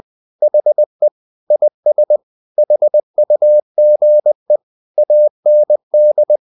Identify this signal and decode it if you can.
Morse code, heard as a keyed tone, about 20 wpm, 600 Hz. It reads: HE IS HUGE AND
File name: the old man and the sea part 2 20wpm